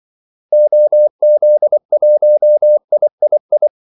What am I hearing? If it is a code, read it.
Morse code, heard as a keyed tone, about 24 wpm, 600 Hz. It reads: OZ1III
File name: Callsigns-24wpm